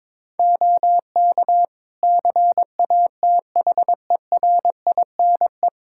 Morse code, heard as a keyed tone, about 21 wpm, 700 Hz. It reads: OK CATHERINE
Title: Practice QSO - Exchange 2 - Info - 22wpm